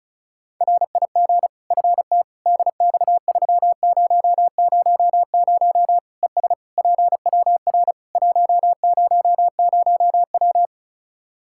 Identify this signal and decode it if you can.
Morse code, heard as a keyed tone, about 36 wpm, 700 Hz. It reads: RIG FT DX3000 ES PWR 100W
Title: Practice QSO - Exchange 2 - Info - 35wpm